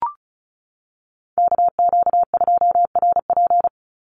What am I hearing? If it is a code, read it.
Morse code, heard as a keyed tone, about 36 wpm, 700 Hz. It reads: KQ2RP
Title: NA SOTA Chaser Callsigns 35wpm